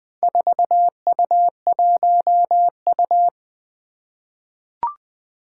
Morse code, heard as a keyed tone, about 20 wpm, 700 Hz. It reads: 4U1U
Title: Four Challenging Letter-Number Combinations 20wpm